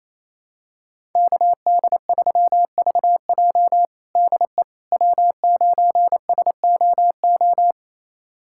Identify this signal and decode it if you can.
Morse code, heard as a keyed tone, about 27 wpm, 700 Hz. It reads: KD3VJ DE W9SOO
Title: USA Callsign DE Callsign - 5 Characters - 28wpm